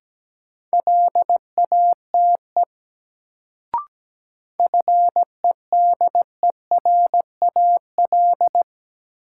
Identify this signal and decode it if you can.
Morse code, heard as a keyed tone, about 17 wpm, 700 Hz. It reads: LATE FEDERAL
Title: Sets of 3 Words - New Word Drill - 17wpm